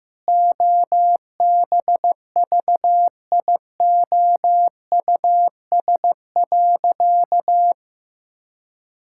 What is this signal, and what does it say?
Morse code, heard as a keyed tone, about 15 wpm, 700 Hz. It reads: OBVIOUS.
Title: Sentences from Top 1600 Words 15wpm